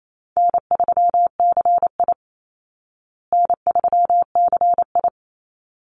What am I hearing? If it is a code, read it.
Morse code, heard as a keyed tone, about 28 wpm, 700 Hz. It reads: N3CI N3CI
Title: Easy Call Signs Repeated 2x - 28wpm